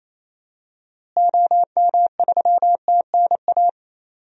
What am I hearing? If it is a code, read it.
Morse code, heard as a keyed tone, about 27 wpm, 700 Hz. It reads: OM3TNA
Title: DXCC-100 Call Signs - 6 Characters - Repeated 3x - 28wpm